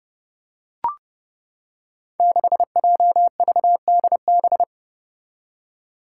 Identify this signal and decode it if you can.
Morse code, heard as a keyed tone, about 31 wpm, 700 Hz. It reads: 6JVDB
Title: Five Challenging Letter-Number Combinations 30wpm